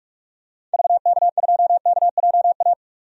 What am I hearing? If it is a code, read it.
Morse code, heard as a keyed tone, about 46 wpm, 700 Hz. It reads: VK2KJA
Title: DXCC-100 Call Signs - 6 Characters - 45wpm